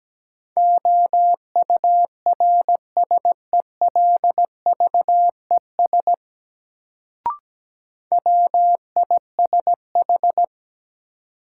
Morse code, heard as a keyed tone, about 17 wpm, 700 Hz. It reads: OURSELVES WISH
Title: Sentences from Top 1600 Words - New Word Drill - 17wpm